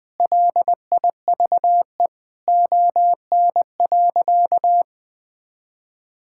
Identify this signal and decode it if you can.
Morse code, heard as a keyed tone, about 20 wpm, 700 Hz. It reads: LIVE ON.
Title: Sentences from Top 1000 Words - Review Part 6 - 20wpm